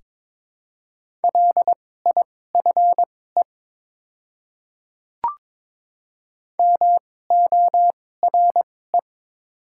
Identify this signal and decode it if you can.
Morse code, heard as a keyed tone, about 22 wpm, spaced as for 16 wpm, 700 Hz. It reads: LIFE MORE
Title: 2x Character Spacing - Top 300 Words - 22wpm